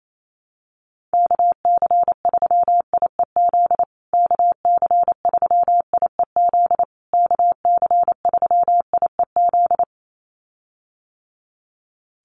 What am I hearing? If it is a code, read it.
Morse code, heard as a keyed tone, about 28 wpm, 700 Hz. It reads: KC3IEZ KC3IEZ KC3IEZ
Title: USA Call Signs Repeated 3x 28wpm